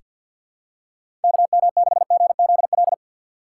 Morse code, heard as a keyed tone, about 48 wpm, 700 Hz. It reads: KM6GZL
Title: USA Call Signs - 6 Characters - 4x Character Spacing - 50wpm